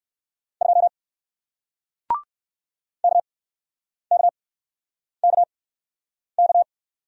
Morse code, heard as a keyed tone, about 60 wpm, spaced as for 9 wpm, 700 Hz. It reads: 2 KKKK